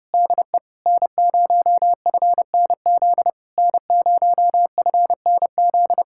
Morse code, heard as a keyed tone, about 33 wpm, 700 Hz. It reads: DE N0FNZ N0FNZ